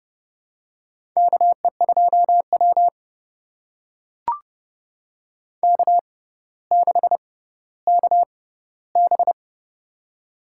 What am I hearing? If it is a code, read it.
Morse code, heard as a keyed tone, about 30 wpm, spaced as for 15 wpm, 700 Hz. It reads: KE2W K6KB
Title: USA Call Signs - 4 Characters - 6x Character Spacing - 30wpm